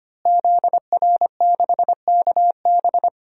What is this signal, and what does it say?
Morse code, heard as a keyed tone, about 25 wpm, 700 Hz. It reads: ZR6KB
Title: DXCC-100 Call Signs - 5 Characters - Repeated 2x - 25wpm